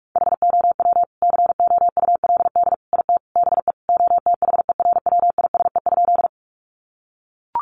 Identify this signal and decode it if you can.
Morse code, heard as a keyed tone, about 44 wpm, 700 Hz. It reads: HOW COULD IT BE OTHERWISE?
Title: Sentences from Top 1200 Words 45wpm